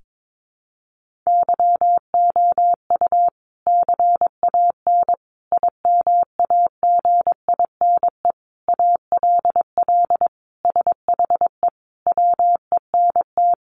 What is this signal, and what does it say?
Morse code, heard as a keyed tone, about 22 wpm, 700 Hz. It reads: YOU CAN IMAGINE ALL SHE WENT
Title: Sentences from Top 1400 Words 22wpm